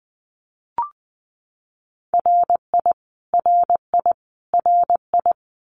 Morse code, heard as a keyed tone, about 20 wpm, 700 Hz. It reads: RI RI RI
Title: US States Repeated 3x 20wpm